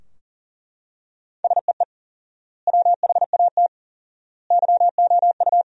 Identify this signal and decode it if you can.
Morse code, heard as a keyed tone, about 40 wpm, 700 Hz. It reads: SEE WHAT YOU
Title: 4x Word-Spacing - Sentences from Top 500 Words - 40wpm